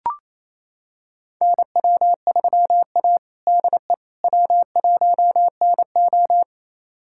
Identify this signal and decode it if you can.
Morse code, heard as a keyed tone, about 28 wpm, 690 Hz. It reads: NW3A DE W1NO
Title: USA Callsign DE Callsign - 4 Characters - 28wpm